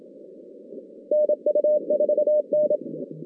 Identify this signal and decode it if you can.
Morse code, heard as a keyed tone, about 27 wpm, 600 Hz. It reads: NU4N
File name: 2016 JAN CW NAQP / 2016 Jan naqp k5ot - 0-9